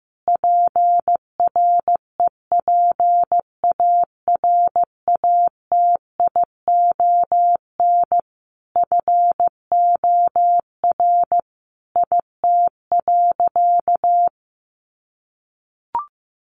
Morse code, heard as a keyed tone, about 15 wpm, 700 Hz. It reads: PREPARATION FOR IT.
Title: Sentences from Top 1400 Words 15wpm